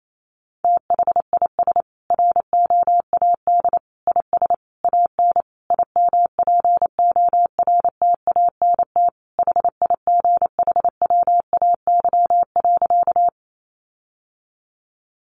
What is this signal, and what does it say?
Morse code, heard as a keyed tone, about 28 wpm, 700 Hz. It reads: THIS ROAD IS AN IMPORTANT HIGHWAY.
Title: Sentences from Top 1400 Words 28wpm